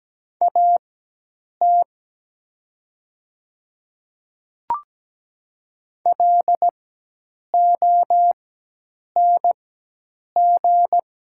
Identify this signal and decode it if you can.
Morse code, heard as a keyed tone, about 17 wpm, spaced as for 8 wpm, 700 Hz. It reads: AT LONG
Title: 4x Character Spacing - Top 100 Words - 17wpm